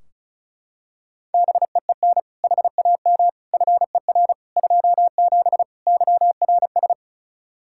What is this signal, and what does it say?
Morse code, heard as a keyed tone, about 36 wpm, 700 Hz. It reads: BEEN HAM FER 27 YRS
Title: K1USN SST QSO Practice 35wpm